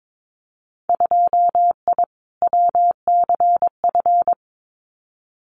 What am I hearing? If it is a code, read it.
Morse code, heard as a keyed tone, about 22 wpm, 700 Hz. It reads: 2I WCF
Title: Winter Field Day Exchanges 22wpm